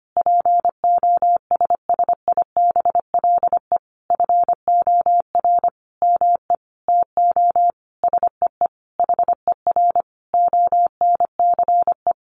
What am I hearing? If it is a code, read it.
Morse code, heard as a keyed tone, about 25 wpm, 700 Hz. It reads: POSSIBLE FOR ME TO SEE HER ONCE
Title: Sentences from Top 500 Words 25wpm